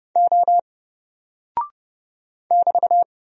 Morse code, heard as a keyed tone, about 30 wpm, 700 Hz. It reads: O <BT>
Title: Single Character (Letter:Number:Punctuation:Prosign) Alt 30wpm